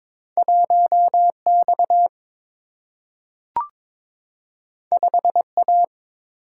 Morse code, heard as a keyed tone, about 22 wpm, 700 Hz. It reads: 1X 5A
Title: Single-Letter Single-Number 22wpm